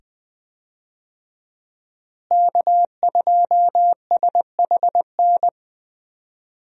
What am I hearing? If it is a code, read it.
Morse code, heard as a keyed tone, about 20 wpm, 700 Hz. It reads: K2SHN
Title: USA Call Signs - 5 Characters - Repeated 2x - 20wpm